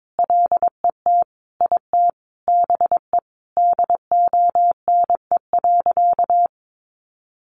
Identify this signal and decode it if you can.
Morse code, heard as a keyed tone, about 22 wpm, 700 Hz. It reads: LET IT BE DONE.
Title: Sentences from Top 500 Words 22wpm